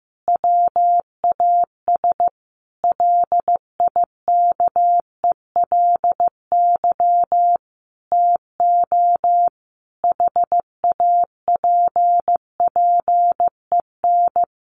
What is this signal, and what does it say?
Morse code, heard as a keyed tone, about 15 wpm, 700 Hz. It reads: WAS LIKELY TO HAPPEN
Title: Sentences from Top 1300 Words 15wpm